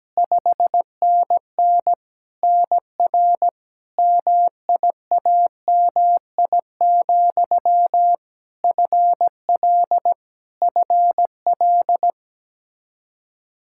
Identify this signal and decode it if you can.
Morse code, heard as a keyed tone, about 17 wpm, 700 Hz. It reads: 5NN NR MIAMI, FL FL
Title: K1USN SST QSO Practice 17wpm